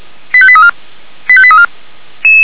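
Five-tone selective call (ZVEI), heard as quick runs of five tones, tones of about 70 ms. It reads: 85834 85834
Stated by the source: ZVEI five-tone call received with an SDR.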